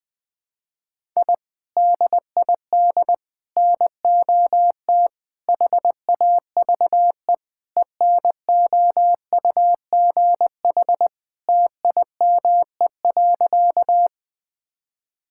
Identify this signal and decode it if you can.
Morse code, heard as a keyed tone, about 20 wpm, 700 Hz. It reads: I DID NOT HAVE ENOUGH TIME.
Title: Sentences with 4-Word Phrases - 20wpm